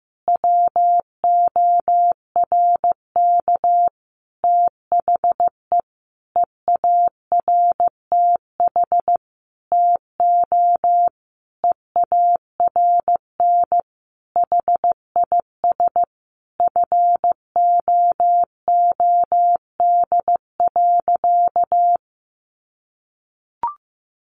Morse code, heard as a keyed tone, about 15 wpm, 700 Hz. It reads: WORK THE EARTH TO EARN HIS FOOD.
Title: Sentences from Top 1500 Words 15wpm